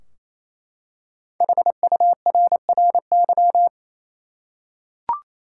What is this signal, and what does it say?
Morse code, heard as a keyed tone, about 28 wpm, 700 Hz. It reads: HURRY
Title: Five Letter Words - 28wpm